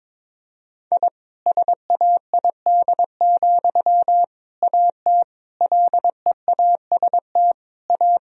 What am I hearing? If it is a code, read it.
Morse code, heard as a keyed tone, about 22 wpm, 700 Hz. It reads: I SAID, AT LEAST A
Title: Sentences from Top 1000 Words - Review Part 2 - 22wpm